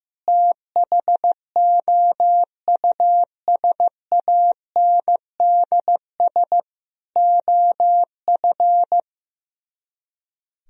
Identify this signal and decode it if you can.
Morse code, heard as a keyed tone, about 15 wpm, 700 Hz. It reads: THOUSANDS OF
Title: Speed-Racing - Sets of 4 Words 15wpm